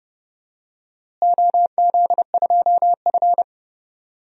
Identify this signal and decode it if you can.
Morse code, heard as a keyed tone, about 32 wpm, 700 Hz. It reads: OZ2F
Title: DXCC-100 Call Signs - 4 Characters - Repeated 3x - 30wpm